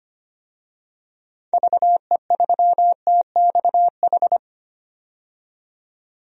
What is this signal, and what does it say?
Morse code, heard as a keyed tone, about 25 wpm, 700 Hz. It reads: VE3TXH
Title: DXCC-100 Call Signs - 6 Characters - 25wpm